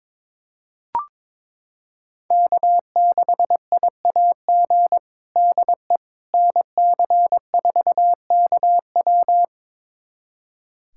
Morse code, heard as a keyed tone, about 22 wpm, 700 Hz. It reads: K6IAG DE NC4KW
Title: USA Callsign DE Callsign - 5 Characters - 22wpm